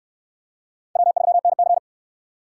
Morse code, heard as a keyed tone, about 62 wpm, 700 Hz. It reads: W3NC